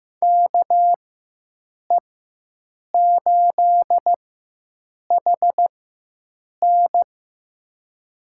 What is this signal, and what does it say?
Morse code, heard as a keyed tone, about 15 wpm, spaced as for 7 wpm, 700 Hz. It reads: KE8HN
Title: USA Call Signs - 4x Character Spacing - 15wpm